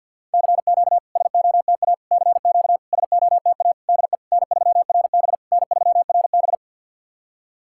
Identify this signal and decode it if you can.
Morse code, heard as a keyed tone, about 48 wpm, 690 Hz. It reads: CQ SOTA CQ SOTA DE N3RB N3RB
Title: SOTA QSO Practice 50wpm